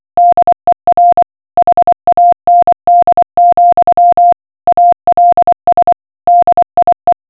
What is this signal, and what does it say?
Morse code, heard as a keyed tone, about 24 wpm, 700 Hz. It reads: DER HAND, ALS DIE